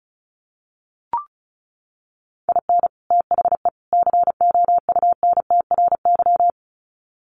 Morse code, heard as a keyed tone, about 34 wpm, 700 Hz. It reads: IN THE COUNTRY
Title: Sets of 3 Words 35wpm